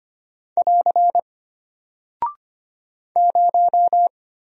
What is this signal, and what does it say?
Morse code, heard as a keyed tone, about 25 wpm, 700 Hz. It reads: <AR> 0
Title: Single Character (Letter:Number:Punctuation:Prosign) Alt 25wpm